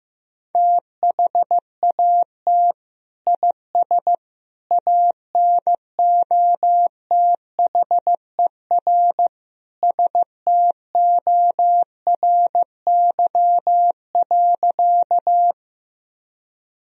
Morse code, heard as a keyed tone, about 15 wpm, 700 Hz. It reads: THAT IS ANOTHER STORY.
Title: Sentences from Top 1000 Words - Review Part 1 - 15wpm